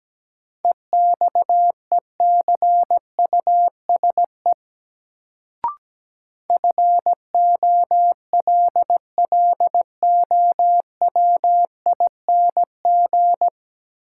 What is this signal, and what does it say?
Morse code, heard as a keyed tone, about 17 wpm, 700 Hz. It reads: EXECUSE FOLLOWING
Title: Sentences from Top 700 Words - New Word Drill - 17wpm